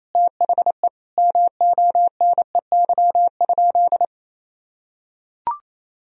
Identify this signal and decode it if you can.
Morse code, heard as a keyed tone, about 27 wpm, 700 Hz. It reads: THE MONEY?
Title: Sentences from Top 1000 Words - Review Part 2 - 28wpm